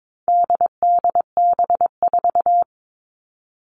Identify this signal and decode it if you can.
Morse code, heard as a keyed tone, about 22 wpm, 700 Hz. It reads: DDB4
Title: All Cut Numbers - 4-Digit 22wpm